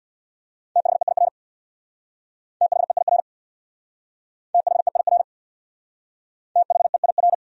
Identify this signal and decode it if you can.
Morse code, heard as a keyed tone, about 68 wpm, 700 Hz. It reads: THEIR THEIR THEIR THEIR